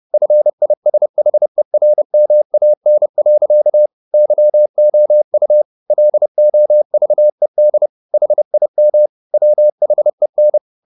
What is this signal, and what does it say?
Morse code, heard as a keyed tone, about 30 wpm, 600 Hz. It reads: FISHERMAN. YOU LOVED HIM WHEN